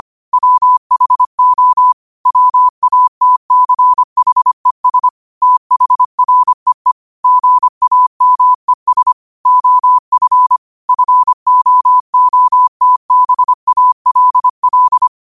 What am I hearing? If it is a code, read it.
Morse code, heard as a keyed tone, about 25 wpm, 1000 Hz. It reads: WHO WATCHES THREE GAMES OF FOOTBALL
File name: qotdfiles / 25wpm